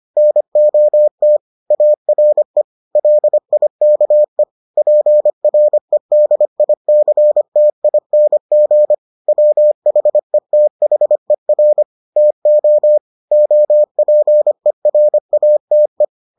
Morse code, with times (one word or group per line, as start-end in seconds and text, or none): 0.2-1.4 NOT
1.7-2.6 ARE
3.0-4.4 LIKE
4.8-8.9 PREDICTING
9.3-11.8 WHETHER
12.2-13.0 TO
13.3-16.0 OPERATE